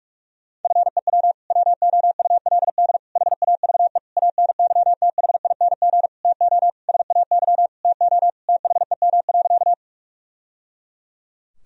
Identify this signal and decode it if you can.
Morse code, heard as a keyed tone, about 41 wpm, 700 Hz. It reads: FEW WOULD HAVE ANYTHING TO SAY TO THEM.
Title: Sentences from Top 1000 Words - Review Part 1 - 45wpm